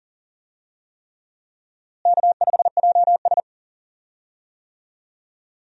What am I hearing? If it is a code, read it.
Morse code, heard as a keyed tone, about 40 wpm, 700 Hz. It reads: K5JS
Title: USA Call Signs - 4 Characters - Repeated 3x - 40wpm